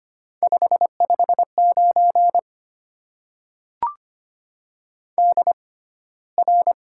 Morse code, heard as a keyed tone, about 25 wpm, spaced as for 15 wpm, 700 Hz. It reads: 559 DR
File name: Top 90 QSO Elements - 6x Character Spacing - 25wpm